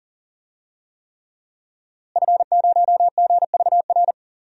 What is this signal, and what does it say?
Morse code, heard as a keyed tone, about 40 wpm, 700 Hz. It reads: F0GVR
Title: DXCC-100 Call Signs - 5 Characters - Repeated 3x - 40wpm